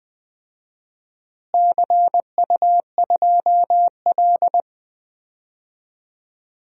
Morse code, heard as a keyed tone, about 20 wpm, 700 Hz. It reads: CU2L
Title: DXCC-100 Call Signs - 4 Characters - Repeated 3x - 20wpm